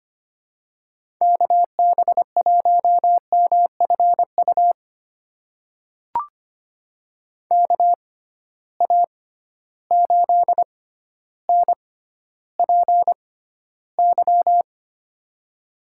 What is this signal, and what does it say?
Morse code, heard as a keyed tone, about 25 wpm, spaced as for 13 wpm, 700 Hz. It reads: KB1MFU KA8NPY
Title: USA Call Signs - 6 Characters - 6x Character Spacing - 25wpm